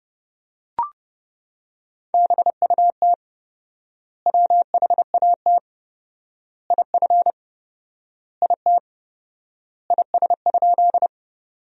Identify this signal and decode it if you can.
Morse code, heard as a keyed tone, about 30 wpm, 700 Hz. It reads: BUT WHAT IF IT IS?
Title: 4x Word-Spacing - Sentences from Top 100 Words - 30wpm